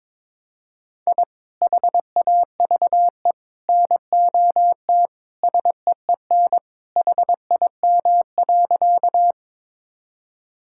Morse code, heard as a keyed tone, about 21 wpm, 700 Hz. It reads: I HAVE NOT SEEN HIM.
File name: practice / 22wpm / Sentences with 3-Word Phrases - 22wpm